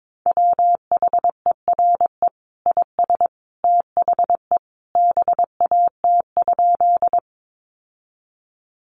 Morse code, heard as a keyed tone, about 22 wpm, 700 Hz. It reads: WHERE IS THE BAT?
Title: Sentences from Top 1500 Words 22wpm